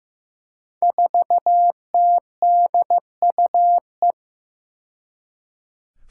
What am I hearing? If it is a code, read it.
Morse code, heard as a keyed tone, about 15 wpm, 700 Hz. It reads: 4TDUE
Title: Five Challenging Letter-Number Combinations 15wpm